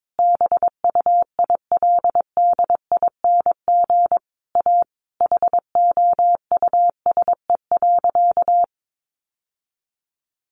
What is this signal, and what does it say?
Morse code, heard as a keyed tone, about 22 wpm, 700 Hz. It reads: BUILDING A HOUSE.
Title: Sentences from Top 700 Words 22wpm